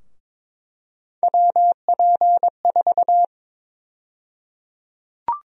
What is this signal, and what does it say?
Morse code, heard as a keyed tone, about 22 wpm, 700 Hz. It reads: WP4
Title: USA Call Sign Prefixes - Repeated 2x - 22wpm